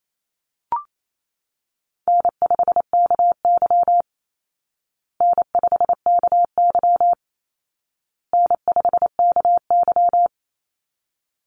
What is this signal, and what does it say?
Morse code, heard as a keyed tone, about 28 wpm, 700 Hz. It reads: N5KY N5KY N5KY
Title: Easy Call Signs Repeated 3x - 28wpm